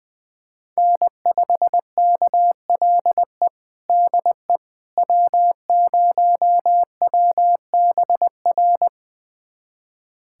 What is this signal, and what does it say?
Morse code, heard as a keyed tone, about 21 wpm, 700 Hz. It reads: N5KLE DE W0WBR
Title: USA Callsign DE Callsign - 5 Characters - 20wpm